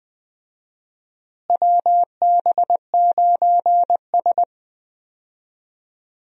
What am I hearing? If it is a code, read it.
Morse code, heard as a keyed tone, about 20 wpm, 700 Hz. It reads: WB9S
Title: USA Call Signs - 4 Characters - Repeated 2x - 20wpm